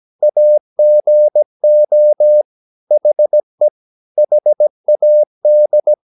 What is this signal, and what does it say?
Morse code, heard as a keyed tone, about 17 wpm, 600 Hz. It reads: AGO HE HAD
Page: Romanen Never af Ken Follet på engelsk.